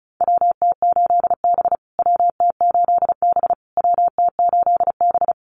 Morse code, heard as a keyed tone, about 34 wpm, 700 Hz. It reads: WT8B WT8B WT8B
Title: USA Call Signs Repeated 3x 35wpm